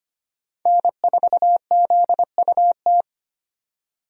Morse code, heard as a keyed tone, about 25 wpm, 700 Hz. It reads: N4ZUT
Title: USA Call Signs - 5 Characters - 25wpm